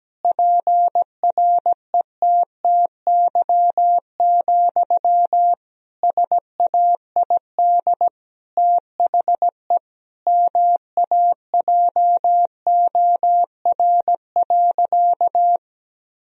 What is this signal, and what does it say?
Morse code, heard as a keyed tone, about 17 wpm, 700 Hz. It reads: PRETTY, SAID THE MAJOR.
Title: Sentences from Top 1000 Words - Review Part 4 - 17wpm